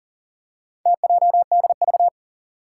Morse code, heard as a keyed tone, about 40 wpm, 700 Hz. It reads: TJDV
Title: Four Challenging Letter-Number Combinations 40wpm